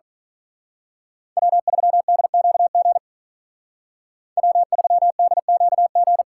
Morse code, heard as a keyed tone, about 43 wpm, 700 Hz. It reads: W3DQG W3DQG
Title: Speed Racing - USA 1x3 Call Signs 35wpm